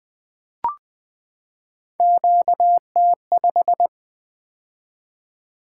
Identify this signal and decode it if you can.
Morse code, heard as a keyed tone, about 20 wpm, 700 Hz. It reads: QT5
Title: Three Letter-Number Combinations - 20wpm